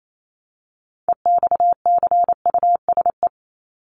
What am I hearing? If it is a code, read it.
Morse code, heard as a keyed tone, about 28 wpm, 700 Hz. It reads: EXCUSE
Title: Top 501-750 Words 28wpm